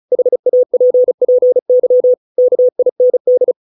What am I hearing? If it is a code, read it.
Morse code, heard as a keyed tone, about 34 wpm, 500 Hz. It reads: HAPPY KIND